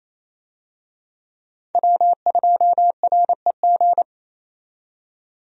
Morse code, heard as a keyed tone, about 27 wpm, 700 Hz. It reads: W2REG
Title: USA Call Signs - 5 Characters - 28wpm